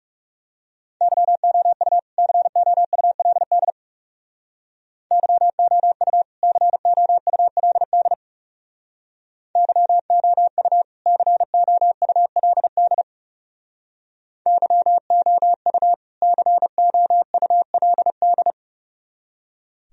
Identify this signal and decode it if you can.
Morse code, heard as a keyed tone, about 40 wpm, 700 Hz. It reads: YOU COULD YOU COULD YOU COULD YOU COULD